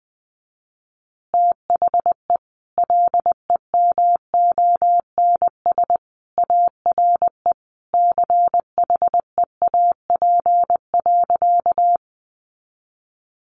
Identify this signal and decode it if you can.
Morse code, heard as a keyed tone, about 20 wpm, 700 Hz. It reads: THE LEMONS ARE CHEAP.
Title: Additional Easy Phrases 20wpm